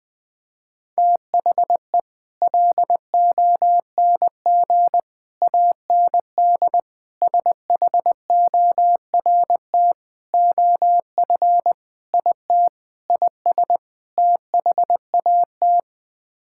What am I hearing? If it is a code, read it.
Morse code, heard as a keyed tone, about 21 wpm, 700 Hz. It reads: THE LONG AND SHORT OF IT IS THAT
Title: Sets of 8 Words - 20wpm